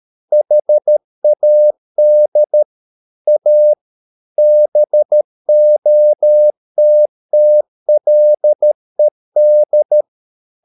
Morse code, heard as a keyed tone, about 13 wpm, 600 Hz. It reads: HAD A BOTTLED
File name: Never 0013wpm